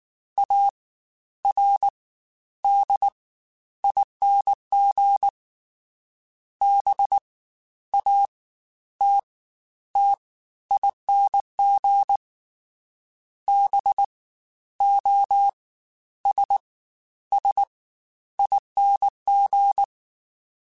Morse code, heard as a keyed tone, about 19 wpm, 800 Hz. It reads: A R D ING B A T T ING B O S S ING